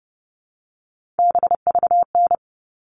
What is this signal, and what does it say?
Morse code, heard as a keyed tone, about 30 wpm, 700 Hz. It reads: BVN
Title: All Cut Numbers - 3-Digit 30wpm